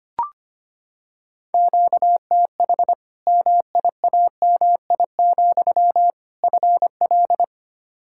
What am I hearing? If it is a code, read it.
Morse code, heard as a keyed tone, about 25 wpm, 700 Hz. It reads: QTH MIAMI, FL
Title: Practice QSO - Exchange 1 - Beginning - 25wpm